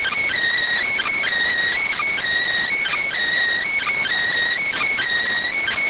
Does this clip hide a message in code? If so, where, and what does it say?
Yes: Emergency Alert System (SAME) tones (header bursts) from 0.8 to 5.0 s, heard as harsh buzzing bursts.